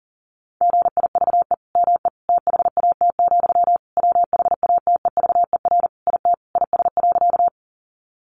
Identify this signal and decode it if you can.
Morse code, heard as a keyed tone, about 40 wpm, 700 Hz. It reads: GIVE ME THAT, WHATEVER IT IS.
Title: Sentences from Top 1100 Words 40wpm